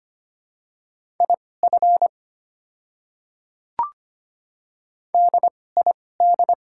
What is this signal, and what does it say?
Morse code, heard as a keyed tone, about 25 wpm, spaced as for 18 wpm, 700 Hz. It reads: IF DID
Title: 2x Character Spacing - Top 200 Words - 25wpm